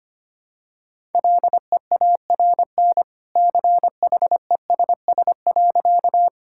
Morse code, heard as a keyed tone, about 25 wpm, 700 Hz. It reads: LEARN CHESS.
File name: Easy Sentences with Hints - 25wpm